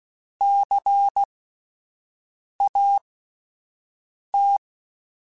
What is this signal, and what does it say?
Morse code, heard as a keyed tone, about 16 wpm, spaced as for 5 wpm, 800 Hz. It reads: CAT
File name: Alphabetical / Alphabetical mp3s / Cs @5